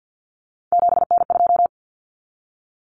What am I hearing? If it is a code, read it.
Morse code, heard as a keyed tone, about 50 wpm, 700 Hz. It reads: 7N2
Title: Common Cut Numbers - 3-Digit 50wpm